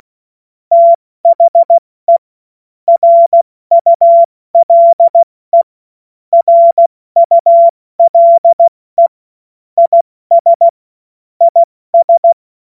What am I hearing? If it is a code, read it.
Morse code, heard as a keyed tone, about 16 wpm, 680 Hz. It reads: THE RULE RULE IS IS
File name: Morsekurs / Witze / Witz 23 Januar 2024 70 F80